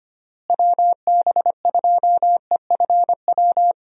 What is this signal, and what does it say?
Morse code, heard as a keyed tone, about 25 wpm, 700 Hz. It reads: WB2EFW
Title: USA Call Signs - 6 Characters - Repeated 3x - 25wpm